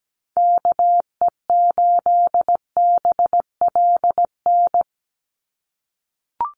Morse code, heard as a keyed tone, about 17 wpm, 700 Hz. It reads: KE8BLN
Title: USA Call Signs 17wpm